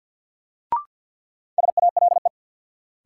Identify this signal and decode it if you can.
Morse code, heard as a keyed tone, about 46 wpm, 700 Hz. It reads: SALE
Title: Sentences from Top 1400 Words - New Word Drill - 50wpm